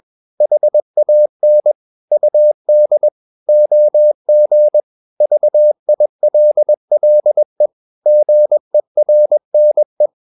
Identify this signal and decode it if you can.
Morse code, heard as a keyed tone, about 21 wpm, 600 Hz. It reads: HAN UD OG VILLE GERNE
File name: Ferie QRS 21wpm